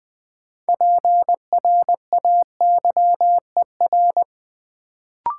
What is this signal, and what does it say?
Morse code, heard as a keyed tone, about 20 wpm, 700 Hz. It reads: PRAYER
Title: Build A Word - 3 Letters At A Time - 6-Letter Words - 20wpm